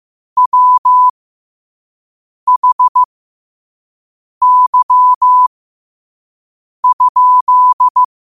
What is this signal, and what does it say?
Morse code, heard as a keyed tone, about 15 wpm, spaced as for 5 wpm, 1000 Hz. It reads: WHY?
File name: qotdfiles / 5wpm